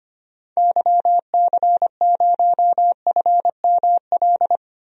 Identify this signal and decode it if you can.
Morse code, heard as a keyed tone, about 25 wpm, 700 Hz. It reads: YC0FML
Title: DXCC-100 Call Signs - 6 Characters - 25wpm